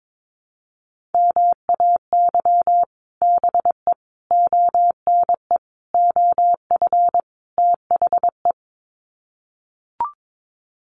Morse code, heard as a keyed tone, about 22 wpm, 700 Hz. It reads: MAY BE ONE OF THE
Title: Sets of 5 Words Encore2 22wpm